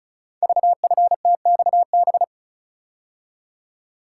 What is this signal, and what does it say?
Morse code, heard as a keyed tone, about 36 wpm, 690 Hz. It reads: VFTXB
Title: Code Groups - 5 Letters - 35wpm